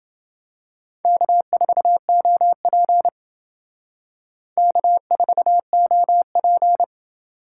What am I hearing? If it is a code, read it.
Morse code, heard as a keyed tone, about 30 wpm, 700 Hz. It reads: K4OP K4OP